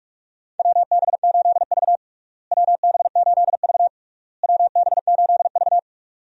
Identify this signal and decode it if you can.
Morse code, heard as a keyed tone, about 46 wpm, 700 Hz. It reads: WB8V WB8V WB8V
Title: USA Call Signs - 4 Characters - Repeated 3x - 45wpm